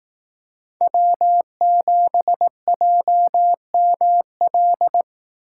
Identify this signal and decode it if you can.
Morse code, heard as a keyed tone, about 18 wpm, 700 Hz. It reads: W7JML
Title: Speed Racing - USA 1x3 Call Signs 15wpm